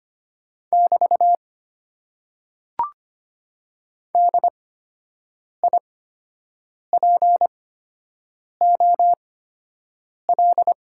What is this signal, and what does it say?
Morse code, heard as a keyed tone, about 25 wpm, spaced as for 7 wpm, 700 Hz. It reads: <BT> DIPOL
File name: Top 90 QSO Elements - 8x Character Spacing - 25wpm